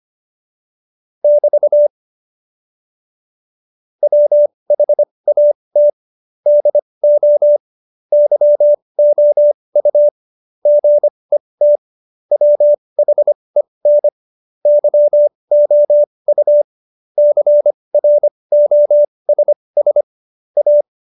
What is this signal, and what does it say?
Morse code, heard as a keyed tone, about 25 wpm, spaced as for 20 wpm, 590 Hz. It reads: <BT> WHAT DO YOU GET WHEN YOU CROSS A
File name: Jokes-25-char-20-eff